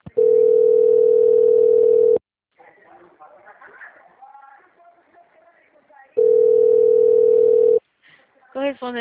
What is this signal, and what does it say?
Ringback tone (North America), 440 + 480 Hz, with speech.
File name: out- - - - - .13713